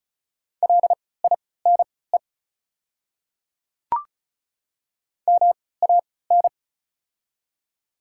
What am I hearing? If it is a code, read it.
Morse code, heard as a keyed tone, about 36 wpm, spaced as for 20 wpm, 700 Hz. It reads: LINE MAN
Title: 3x Character Spacing - Top 200 Words - 35wpm